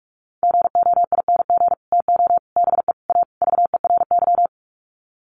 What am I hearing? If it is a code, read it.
Morse code, heard as a keyed tone, about 44 wpm, 700 Hz. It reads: GOING TO BE A VERY
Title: Sets of 5 Words Encore 45wpm